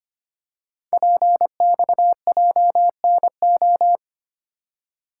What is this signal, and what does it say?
Morse code, heard as a keyed tone, about 25 wpm, 700 Hz. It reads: PXJNO
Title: Code Groups - 5 Letters - 25wpm